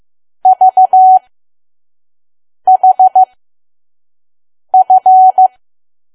Morse code, heard as a keyed tone, about 15 wpm, spaced as for 5 wpm, 750 Hz. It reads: VHF
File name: Archive / 5 WPM files / PM